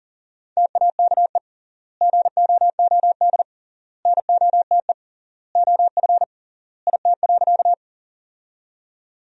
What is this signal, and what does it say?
Morse code, heard as a keyed tone, about 40 wpm, 700 Hz. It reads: TAKE GOOD NOTE OF IT.
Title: 3x Word-Spacing - Sentences from Top 500 Words - 40wpm